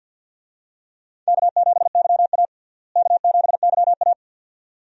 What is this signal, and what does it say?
Morse code, heard as a keyed tone, about 51 wpm, 700 Hz. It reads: K7YA K7YA
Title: USA Call Signs - 4 Characters - Repeated 2x - 50wpm